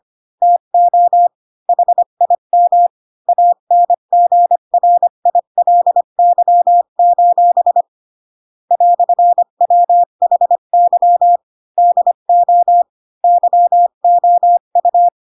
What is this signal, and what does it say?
Morse code, heard as a keyed tone, about 25 wpm, 700 Hz. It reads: TO HIM ANGRILY: "WHY DO YOU